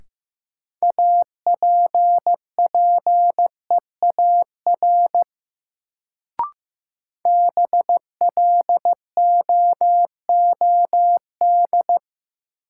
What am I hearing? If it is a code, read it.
Morse code, heard as a keyed tone, about 15 wpm, 700 Hz. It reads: APPEAR BLOOD
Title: Sentences from Top 1000 Words - Review Part 2 - New Word Drill - 15wpm